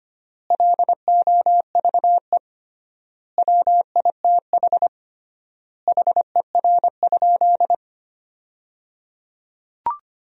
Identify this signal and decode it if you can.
Morse code, heard as a keyed tone, about 25 wpm, 700 Hz. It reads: LOVE WITH HER?
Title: 3x Word-Spacing - Sentences from Top 400 Words - 25wpm